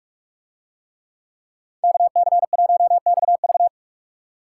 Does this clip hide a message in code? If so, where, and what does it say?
Yes: Morse code from 1.8 to 3.7 s, heard as a keyed tone, about 45 wpm, 700 Hz. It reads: KC1XV